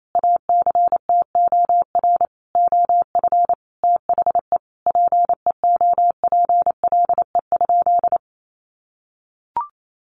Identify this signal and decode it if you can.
Morse code, heard as a keyed tone, about 28 wpm, 700 Hz. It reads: ACTOR OF THE PEOPLE?
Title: Sentences from Top 1500 Words 28wpm